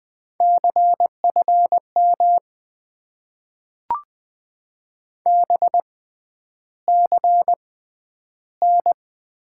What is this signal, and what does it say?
Morse code, heard as a keyed tone, about 20 wpm, spaced as for 10 wpm, 700 Hz. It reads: CFM BCN
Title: Top 90 QSO Elements - 6x Character Spacing - 20wpm